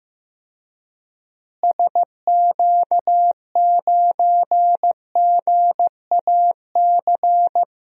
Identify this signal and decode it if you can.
Morse code, heard as a keyed tone, about 15 wpm, 700 Hz. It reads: SQ9GAC
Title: DXCC-100 Call Signs - 6 Characters - Repeated 3x - 15wpm